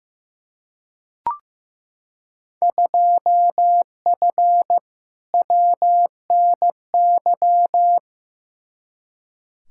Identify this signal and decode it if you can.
Morse code, heard as a keyed tone, about 15 wpm, 700 Hz. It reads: 2F WNY
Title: Field Day Exchanges Encore - 15wpm